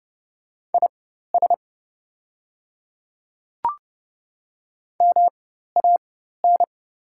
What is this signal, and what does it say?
Morse code, heard as a keyed tone, about 30 wpm, spaced as for 14 wpm, 700 Hz. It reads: IS MAN
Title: 4x Character Spacing - Top 200 Words - 30wpm